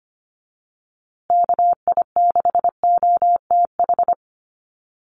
Morse code, heard as a keyed tone, about 25 wpm, 700 Hz. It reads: KI6OTH
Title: USA Call Signs Repeated 2x 25wpm